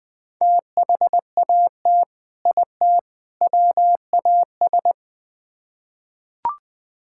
Morse code, heard as a keyed tone, about 21 wpm, 700 Hz. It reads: THAT IT WAS
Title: Build A Phrase - 3-Word Phrases - 20wpm